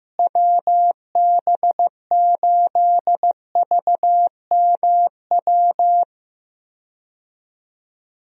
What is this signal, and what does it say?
Morse code, heard as a keyed tone, about 15 wpm, 700 Hz. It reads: WB8VMW
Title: USA Call Signs - 6 Characters - Repeated 2x - 15wpm